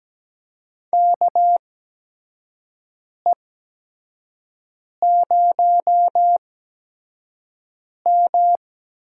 Morse code, heard as a keyed tone, about 17 wpm, spaced as for 5 wpm, 700 Hz. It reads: KE0M
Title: USA Call Signs - 6 Characters - 8x Character Spacing - 17wpm